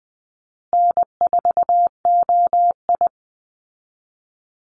Morse code, heard as a keyed tone, about 20 wpm, 700 Hz. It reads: N4OI
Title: Easy Call Signs Repeated 2x - 20wpm